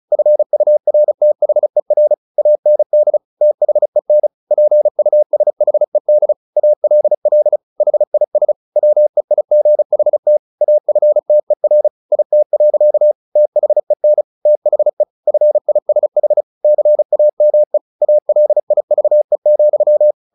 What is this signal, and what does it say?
Morse code, heard as a keyed tone, about 34 wpm, 600 Hz. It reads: FURTHER AND THEN PUSHED ALL HIS WEIGHT AFTER IT. THEN THE FISH CAME ALIVE,
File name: the old man and the sea 2 35wpm